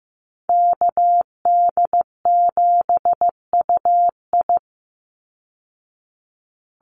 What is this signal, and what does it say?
Morse code, heard as a keyed tone, about 15 wpm, 700 Hz. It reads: KD7UI
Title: Call Signs Repeated 2x - 15wpm